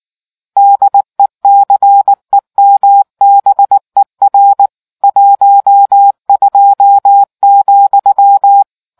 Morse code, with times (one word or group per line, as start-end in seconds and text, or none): 0.6-4.7 DECEMBER
5.0-8.6 12,